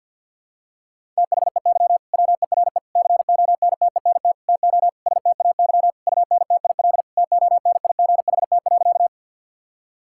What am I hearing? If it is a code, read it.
Morse code, heard as a keyed tone, about 50 wpm, 700 Hz. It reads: THEY WERE CONTENT TO STAY UNTIL TONIGHT.
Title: Sentences from Top 1000 Words - Review Part 6 - 50wpm